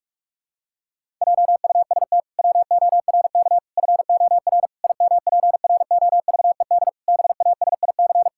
Morse code, heard as a keyed tone, about 46 wpm, 700 Hz. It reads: JUST WORK FOR IMPROVED BASIC
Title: Pangram Challenge v2 - 45wpm